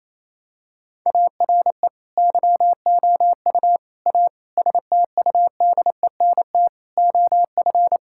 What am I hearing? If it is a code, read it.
Morse code, heard as a keyed tone, about 28 wpm, 700 Hz. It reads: ARE YOU A STUDENT OF
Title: Sentences from Top 1000 Words - Review Part 6 - 28wpm